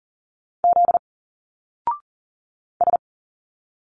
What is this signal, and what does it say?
Morse code, heard as a keyed tone, about 40 wpm, 700 Hz. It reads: Z S
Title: Single Letters 40wpm